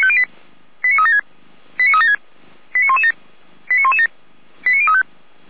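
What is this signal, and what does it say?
Five-tone selective call (ZVEI), heard as quick runs of five tones, tones of about 80 ms. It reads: 89376 89376 89117 89117 89035
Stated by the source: Five-tone ZVEI selective call 89084, received on an SDR.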